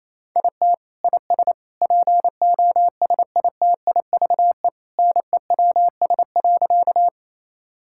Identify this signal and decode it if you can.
Morse code, heard as a keyed tone, about 27 wpm, 700 Hz. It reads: IT IS POSITIVE NEWS.
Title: Sentences from Top 1000 Words - Review Part 5 - 28wpm